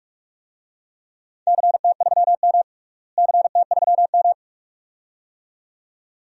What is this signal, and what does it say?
Morse code, heard as a keyed tone, about 46 wpm, 700 Hz. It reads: CT3M CT3M
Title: DXCC-100 Call Signs - 4 Characters - Repeated 2x - 45wpm